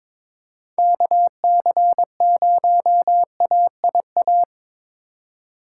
Morse code, heard as a keyed tone, about 22 wpm, 700 Hz. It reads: KC0AIA
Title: USA Call Signs - 6 Characters - 2x Character Spacing - 22wpm